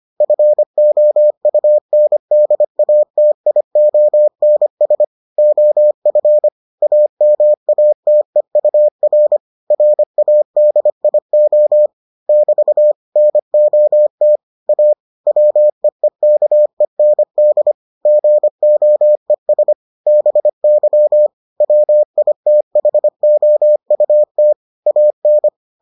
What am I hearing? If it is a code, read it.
Morse code, heard as a keyed tone, about 25 wpm, 600 Hz. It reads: FOUNDATIONS OF AMATEUR RADIO <BT> NOT A WEEKEND GOES BY WITHOUT AN